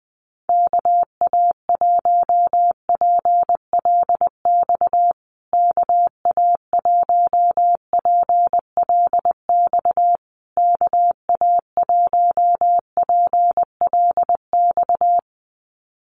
Morse code, heard as a keyed tone, about 20 wpm, 700 Hz. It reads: KA1PLX KA1PLX KA1PLX
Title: USA Call Signs Repeated 3x 20wpm